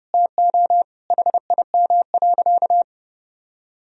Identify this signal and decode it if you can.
Morse code, heard as a keyed tone, about 30 wpm, 700 Hz. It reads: TO HIM.
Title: Sentences from Top 1000 Words - Review Part 2 - 30wpm